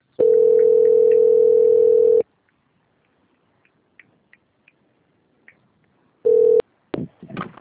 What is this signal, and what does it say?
Ringback tone (North America), 440 + 480 Hz.
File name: out- - - - - .1970